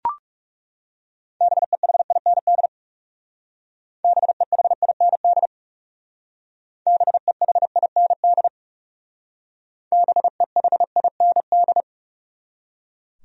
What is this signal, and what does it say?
Morse code, heard as a keyed tone, about 38 wpm, 700 Hz. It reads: BEHIND BEHIND BEHIND BEHIND